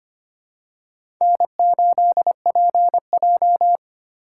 Morse code, heard as a keyed tone, about 25 wpm, 700 Hz. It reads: N8PJ
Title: USA Call Signs - 3x Character Spacing - 25wpm